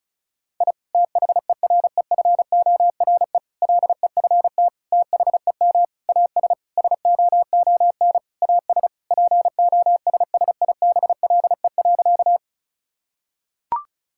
Morse code, heard as a keyed tone, about 34 wpm, 700 Hz. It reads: I THEREFORE LEFT THEM AS SOON AS POSSIBLE.
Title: Sentences from Top 1000 Words - Review Part 5 - 35wpm